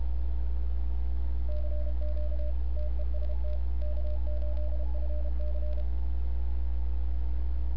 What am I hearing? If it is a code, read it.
Morse code, heard as a keyed tone, about 32 wpm, 600 Hz. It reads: CQ TEST K8FC
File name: n6tr / XE1RCS 1025UTC